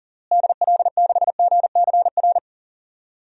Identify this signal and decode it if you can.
Morse code, heard as a keyed tone, about 40 wpm, 700 Hz. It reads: DL6GCR
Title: DXCC-100 Call Signs - 6 Characters - Repeated 3x - 40wpm